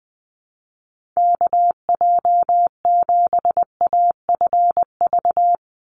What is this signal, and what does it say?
Morse code, heard as a keyed tone, about 20 wpm, 700 Hz. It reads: KJ7AFV
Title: USA Call Signs 20wpm